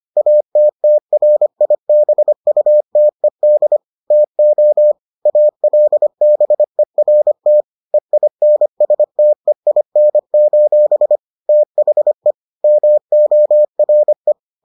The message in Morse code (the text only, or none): ATTRIBUTED TO ALBERT EINSTEIN: THE MORE